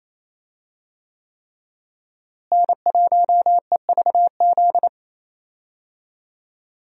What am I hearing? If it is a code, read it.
Morse code, heard as a keyed tone, about 28 wpm, 700 Hz. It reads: N1EVZ
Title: USA Call Signs - 5 Characters - Repeated 3x - 28wpm